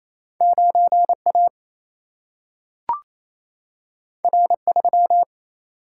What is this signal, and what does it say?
Morse code, heard as a keyed tone, about 28 wpm, 700 Hz. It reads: 9 A R 3
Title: Single-Letter Single-Number 28wpm